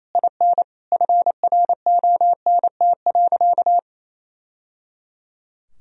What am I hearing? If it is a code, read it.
Morse code, heard as a keyed tone, about 27 wpm, 700 Hz. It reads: IN FRONT.
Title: Sentences from Top 1000 Words - Review Part 2 - 28wpm